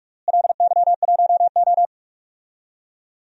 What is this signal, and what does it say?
Morse code, heard as a keyed tone, about 41 wpm, 700 Hz. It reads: LY1O
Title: DXCC-100 Call Signs - 4 Characters - Repeated 3x - 45wpm